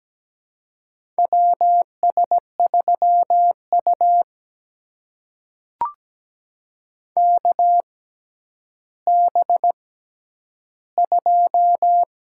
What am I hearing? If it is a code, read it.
Morse code, heard as a keyed tone, about 17 wpm, spaced as for 10 wpm, 700 Hz. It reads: WS3U KB2
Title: USA Call Signs - 6x Character Spacing - 17wpm